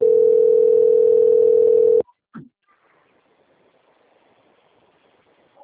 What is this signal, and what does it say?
Ringback tone (North America), 440 + 480 Hz.